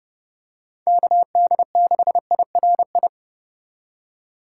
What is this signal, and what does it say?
Morse code, heard as a keyed tone, about 30 wpm, 700 Hz. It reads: KD6IRI
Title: K1USN SST QSO Practice - 30wpm